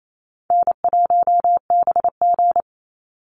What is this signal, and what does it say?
Morse code, heard as a keyed tone, about 28 wpm, 700 Hz. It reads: N1BG
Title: Easy Call Signs Repeated 3x - 28wpm